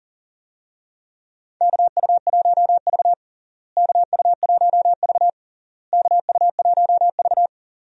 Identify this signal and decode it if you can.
Morse code, heard as a keyed tone, about 40 wpm, 700 Hz. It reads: KU1V KU1V KU1V
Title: USA Call Signs - 4 Characters - Repeated 3x - 40wpm